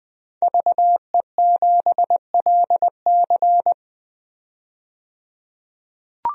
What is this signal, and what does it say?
Morse code, heard as a keyed tone, about 21 wpm, 700 Hz. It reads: VE7LC
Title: DXCC-100 Call Signs - 5 Characters - 20wpm